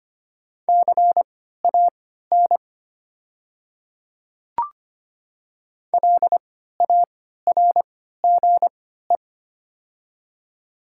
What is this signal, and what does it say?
Morse code, heard as a keyed tone, about 25 wpm, spaced as for 14 wpm, 690 Hz. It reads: CAN LARGE
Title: 3x Character Spacing - Top 200 Words - 25wpm